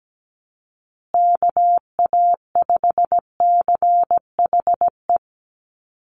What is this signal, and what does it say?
Morse code, heard as a keyed tone, about 17 wpm, 700 Hz. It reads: KA5CHE
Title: USA Call Signs Repeated 3x 17wpm